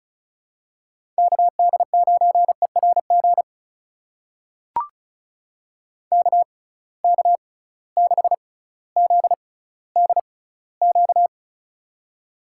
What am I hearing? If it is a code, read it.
Morse code, heard as a keyed tone, about 36 wpm, spaced as for 18 wpm, 700 Hz. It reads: KD9ERG KK6ZDQ
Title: USA Call Signs - 6 Characters - 6x Character Spacing - 35wpm